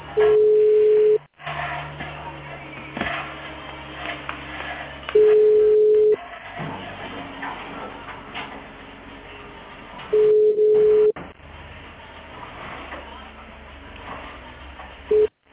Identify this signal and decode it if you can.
Ringback tone (Europe, 425 Hz).